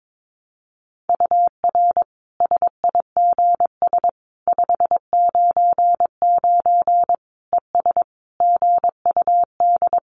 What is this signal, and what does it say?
Morse code, heard as a keyed tone, about 22 wpm, 700 Hz. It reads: UR SIGS 599 ES GUD
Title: Common QSO Phrases 22wpm